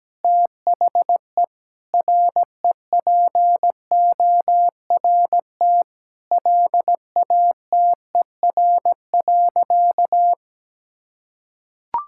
Morse code, heard as a keyed tone, about 17 wpm, 700 Hz. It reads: THE REPORT LATER.
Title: Sentences from Top 1000 Words - Review Part 6 - 17wpm